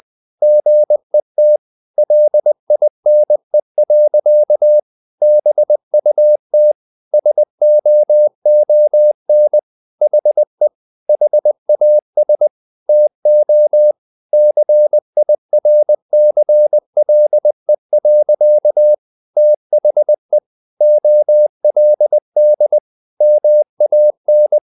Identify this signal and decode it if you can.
Morse code, heard as a keyed tone, about 20 wpm, 600 Hz. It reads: GET LINE. BUT SOON HE HAS TO CIRCLE. THE OLD MAN
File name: the old man and the sea part 2 20wpm